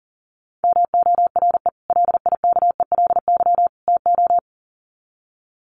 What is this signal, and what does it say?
Morse code, heard as a keyed tone, about 40 wpm, 700 Hz. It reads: MORE LIKELY TO
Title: Sets of 3 Words 40wpm